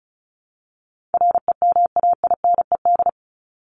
Morse code, heard as a keyed tone, about 34 wpm, 700 Hz. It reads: REMAINED
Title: ED Suffix - Top 100 Words with Base Words 35wpm